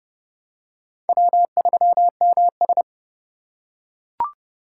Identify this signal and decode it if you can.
Morse code, heard as a keyed tone, about 29 wpm, 700 Hz. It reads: W3MS